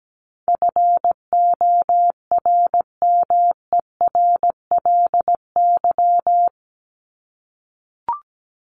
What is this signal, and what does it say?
Morse code, heard as a keyed tone, about 17 wpm, 700 Hz. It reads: FORMERLY
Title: LY Suffix - Top 100 Words with Base Words 17wpm